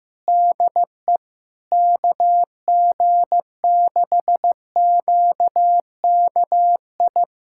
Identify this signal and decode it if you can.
Morse code, heard as a keyed tone, about 15 wpm, 700 Hz. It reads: DE KG6QKI
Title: USA Callsign DE Callsign 15wpm